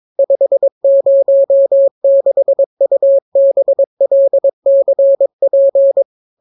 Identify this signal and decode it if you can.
Morse code, heard as a keyed tone, about 22 wpm, 550 Hz. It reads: 506UBLCP